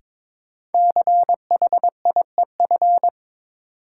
Morse code, heard as a keyed tone, about 22 wpm, 700 Hz. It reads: CHIEF
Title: Sets of 5 Words Encore - New Word Drill - 22wpm